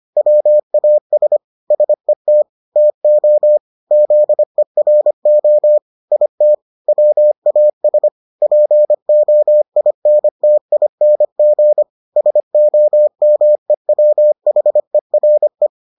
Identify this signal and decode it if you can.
Morse code, heard as a keyed tone, about 25 wpm, 600 Hz. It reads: WAS SET TO ZERO IT WAS POINTING SOMEWHERE